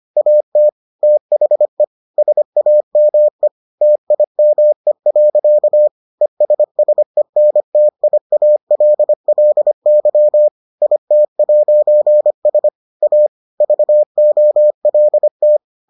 Morse code, heard as a keyed tone, about 25 wpm, 600 Hz. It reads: AT THE SAME TIME. ESSENTIALLY IT'S A VOLT